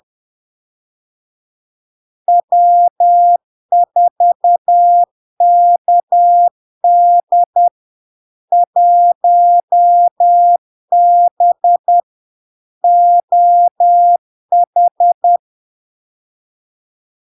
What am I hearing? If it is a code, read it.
Morse code, heard as a keyed tone, about 10 wpm, 700 Hz. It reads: W4KD 1B OH
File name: LARK-FD-2022-Log-in-CW-10wpm-700-Hz